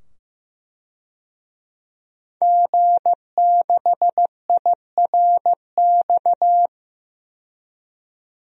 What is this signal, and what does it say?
Morse code, heard as a keyed tone, about 15 wpm, 700 Hz. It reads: G6IRX
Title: DXCC-100 Call Signs - 5 Characters - 15wpm